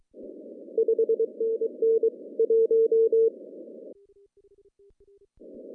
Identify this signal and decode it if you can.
Morse code, heard as a keyed tone, about 23 wpm, 450 Hz. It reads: 5NN 1
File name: CW / cqww-cw-k5tr